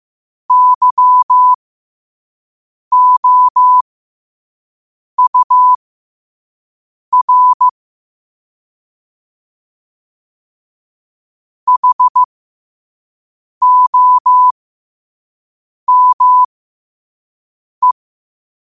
Morse code, heard as a keyed tone, about 15 wpm, spaced as for 5 wpm, 1000 Hz. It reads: YOUR HOME
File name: qotdfiles / 5wpm